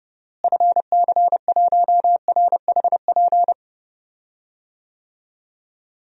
Morse code, heard as a keyed tone, about 32 wpm, 700 Hz. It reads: FC1RHP
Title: DXCC-100 Call Signs - 6 Characters - Repeated 3x - 30wpm